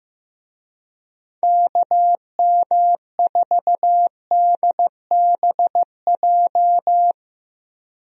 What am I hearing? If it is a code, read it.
Morse code, heard as a keyed tone, about 15 wpm, 700 Hz. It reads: KM4DBJ
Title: USA Call Signs - 6 Characters - Repeated 2x - 15wpm